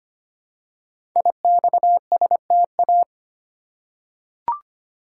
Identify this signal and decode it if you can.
Morse code, heard as a keyed tone, about 25 wpm, 700 Hz. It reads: IXSTA
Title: Code Groups - 5 Letters - 25wpm